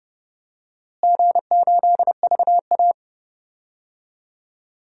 Morse code, heard as a keyed tone, about 30 wpm, 700 Hz. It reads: G8VA
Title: DXCC-100 Call Signs - 4 Characters - Repeated 3x - 30wpm